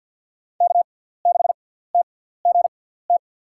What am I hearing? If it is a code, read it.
Morse code, heard as a keyed tone, about 48 wpm, spaced as for 17 wpm, 700 Hz. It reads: K6TGT
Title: USA Call Signs - 6x Character Spacing - 50wpm